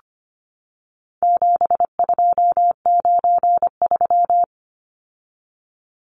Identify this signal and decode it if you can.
Morse code, heard as a keyed tone, about 25 wpm, 700 Hz. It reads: 7293
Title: Numbers - 4-Digit 25wpm